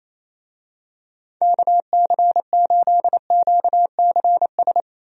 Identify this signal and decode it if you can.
Morse code, heard as a keyed tone, about 28 wpm, 700 Hz. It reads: KC8QCS
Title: USA Call Signs - 6 Characters - Repeated 3x - 28wpm